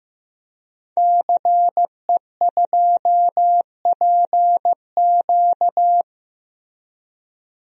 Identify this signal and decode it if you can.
Morse code, heard as a keyed tone, about 16 wpm, 700 Hz. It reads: CE2PQ
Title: DXCC-100 Call Signs - 5 Characters - Repeated 2x - 15wpm